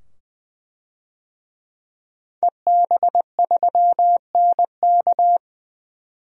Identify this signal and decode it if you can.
Morse code, heard as a keyed tone, about 21 wpm, 700 Hz. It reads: EB3NK
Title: DXCC-100 Call Signs - 5 Characters - 20wpm